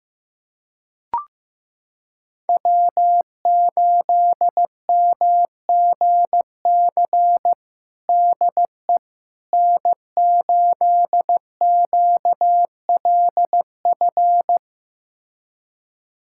Morse code, heard as a keyed tone, about 15 wpm, 700 Hz. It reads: W8MGC DE N8QLF
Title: USA Callsign DE Callsign - 5 Characters - 15wpm